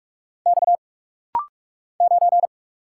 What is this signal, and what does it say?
Morse code, heard as a keyed tone, about 46 wpm, 700 Hz. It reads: X 9
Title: Single Letter-Number Alt - Rapid-Fire 45wpm